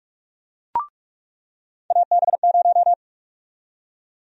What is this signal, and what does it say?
Morse code, heard as a keyed tone, about 46 wpm, 700 Hz. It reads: AB0
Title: USA Call Sign Prefixes - 45wpm